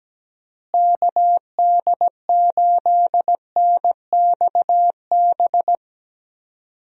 Morse code, heard as a keyed tone, about 17 wpm, 700 Hz. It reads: KD8NXB
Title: USA Call Signs - 6 Characters - Repeated 3x - 17wpm